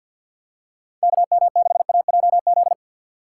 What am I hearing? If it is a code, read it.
Morse code, heard as a keyed tone, about 53 wpm, 700 Hz. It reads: KM6AJZ
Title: USA Call Signs - 2x Character Spacing - 50wpm